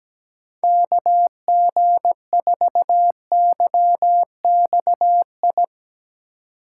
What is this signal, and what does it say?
Morse code, heard as a keyed tone, about 18 wpm, 700 Hz. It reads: KG4YXI
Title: Speed Racing - USA 2x3 Call Signs 17wpm